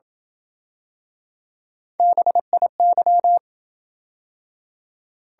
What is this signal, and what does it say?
Morse code, heard as a keyed tone, about 28 wpm, 700 Hz. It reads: BIY